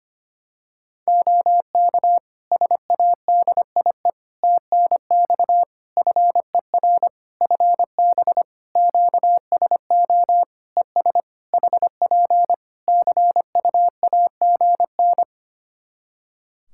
Morse code, heard as a keyed tone, about 25 wpm, 700 Hz. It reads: OK SADIE TNX FER FB QSO ES HP CUAGN
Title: Practice QSO - Exchange 4 - Ending - 25wpm